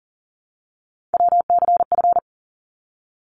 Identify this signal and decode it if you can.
Morse code, heard as a keyed tone, about 40 wpm, 700 Hz. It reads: WCF
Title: ARRL Sections 40wpm